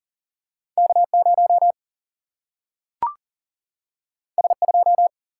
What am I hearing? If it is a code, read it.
Morse code, heard as a keyed tone, about 39 wpm, 700 Hz. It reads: K0 S2
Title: Single-Letter Single-Number 40wpm